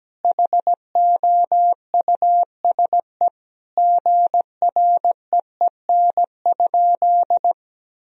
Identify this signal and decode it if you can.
Morse code, heard as a keyed tone, about 17 wpm, 700 Hz. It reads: HOUSE GREEN?
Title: Sentences from Top 1000 Words - Review Part 2 - 17wpm